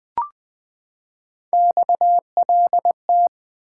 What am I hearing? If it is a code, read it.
Morse code, heard as a keyed tone, about 20 wpm, 700 Hz. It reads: XLT
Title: Three Letter-Number Combinations - 20wpm